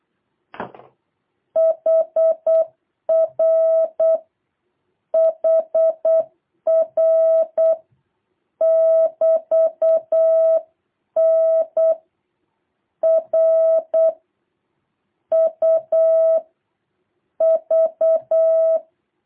Morse code, heard as a keyed tone, about 8 wpm, 650 Hz. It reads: HR HR <BT> N R U V
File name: jh8cbh / 2023cw